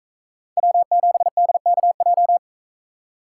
Morse code, heard as a keyed tone, about 44 wpm, 700 Hz. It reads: W7DKJ